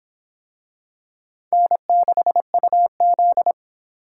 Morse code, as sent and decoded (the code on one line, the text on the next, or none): -. -.... ..- --..
N6UZ